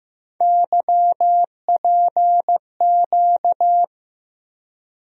Morse code, heard as a keyed tone, about 15 wpm, 700 Hz. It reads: YPQ
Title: USA Call Sign Postfixes - 3 Letters 15wpm